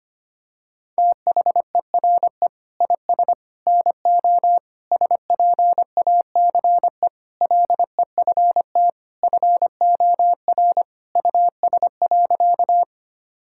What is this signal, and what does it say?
Morse code, heard as a keyed tone, about 25 wpm, 700 Hz. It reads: THERE IS NO SPACE LEFT FOR US.
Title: Sentences from Top 1000 Words - Review Part 5 - 25wpm